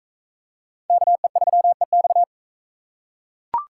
Morse code, heard as a keyed tone, about 36 wpm, 700 Hz. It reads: KE3EX